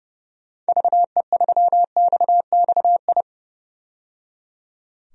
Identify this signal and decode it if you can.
Morse code, heard as a keyed tone, about 30 wpm, 700 Hz. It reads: VE3XXI
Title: DXCC-100 Call Signs - 6 Characters - 30wpm